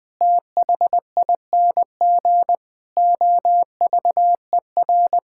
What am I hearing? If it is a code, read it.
Morse code, heard as a keyed tone, about 20 wpm, 700 Hz. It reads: THING OVER
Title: Sentences with 6-Word Phrases - 20wpm